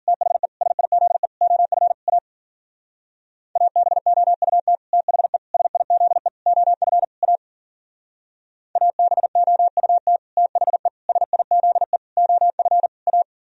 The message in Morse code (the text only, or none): THE SIZE OF A ABOUT THE SIZE OF A ABOUT THE SIZE OF A